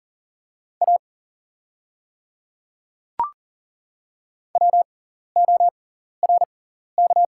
Morse code, heard as a keyed tone, about 40 wpm, spaced as for 14 wpm, 690 Hz. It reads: A WORK
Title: 6x Character Spacing - Top 200 Words - 40wpm